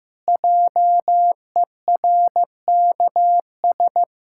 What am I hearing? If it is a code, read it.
Morse code, heard as a keyed tone, about 15 wpm, 700 Hz. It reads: JERKS
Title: Pangram Challenge v2 - 15wpm